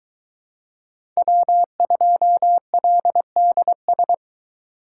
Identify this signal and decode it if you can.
Morse code, heard as a keyed tone, about 22 wpm, 700 Hz. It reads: W2LDS